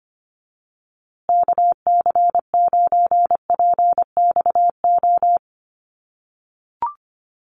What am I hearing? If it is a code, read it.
Morse code, heard as a keyed tone, about 25 wpm, 700 Hz. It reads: KC9PXO
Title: USA Call Signs Repeated 3x 25wpm